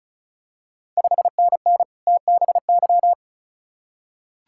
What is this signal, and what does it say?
Morse code, heard as a keyed tone, about 36 wpm, 700 Hz. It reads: 5NN TBY
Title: Ontario QSO Party Exchange 35wpm